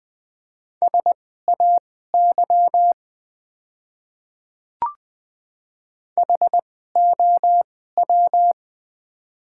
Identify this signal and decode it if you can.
Morse code, heard as a keyed tone, about 20 wpm, spaced as for 14 wpm, 700 Hz. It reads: SAY HOW
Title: 2x Character Spacing - Top 200 Words - 20wpm